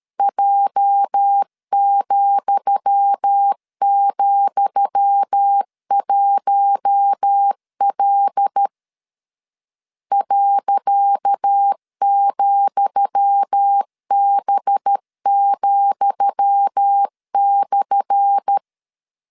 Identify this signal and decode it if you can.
Morse code, heard as a keyed tone, about 13 wpm, 800 Hz. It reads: J,,1L .,B,/
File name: LEZIONI / 10 SIGNS / 10D COMA 12wpm 2544